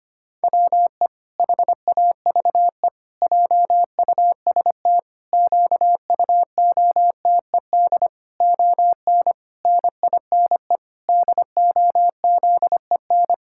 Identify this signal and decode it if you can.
Morse code, heard as a keyed tone, about 25 wpm, 700 Hz. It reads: WE HAVE JUST QUOTED ON NINE DOZEN
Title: Pangram Challenge v2 - 25wpm